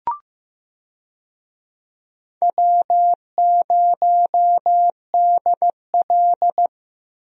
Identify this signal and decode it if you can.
Morse code, heard as a keyed tone, about 15 wpm, 700 Hz. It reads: W0DL
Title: USA Call Signs - 4 Characters - Repeated 2x - 15wpm